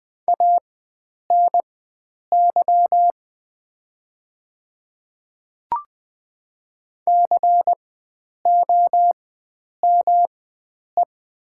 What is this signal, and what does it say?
Morse code, heard as a keyed tone, about 20 wpm, spaced as for 9 wpm, 700 Hz. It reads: ANY COME
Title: 4x Character Spacing - Top 300 Words - 20wpm